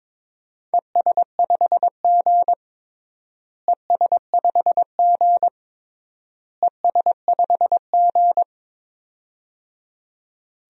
Morse code, heard as a keyed tone, about 22 wpm, 700 Hz. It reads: ES5G ES5G ES5G
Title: DXCC-100 Call Signs - 4 Characters - Repeated 3x - 22wpm